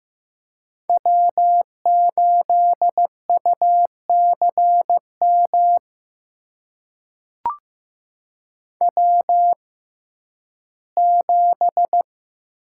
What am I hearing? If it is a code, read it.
Morse code, heard as a keyed tone, about 15 wpm, spaced as for 11 wpm, 700 Hz. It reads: W8UCM W7
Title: USA Call Signs - 5 Characters - 6x Character Spacing - 15wpm